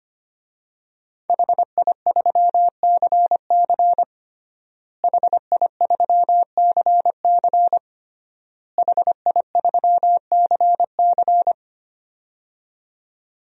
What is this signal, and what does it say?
Morse code, heard as a keyed tone, about 25 wpm, 700 Hz. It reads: HI3CC HI3CC HI3CC
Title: DXCC-100 Call Signs - 5 Characters - Repeated 3x - 25wpm